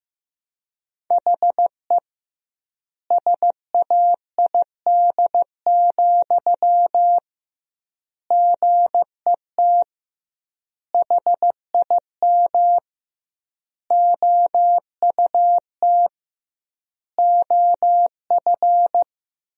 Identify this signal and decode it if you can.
Morse code, heard as a keyed tone, about 16 wpm, 700 Hz. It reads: HE SAID, GET HIM OUT OF
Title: 2x Word-Spacing - Sentences from Top 400 Words - 15wpm